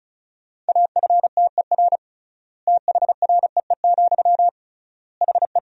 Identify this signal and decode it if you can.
Morse code, heard as a keyed tone, about 36 wpm, 700 Hz. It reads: AFTER THREE, HE
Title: 3x Word-Spacing - Sentences from Top 200 Words - 35wpm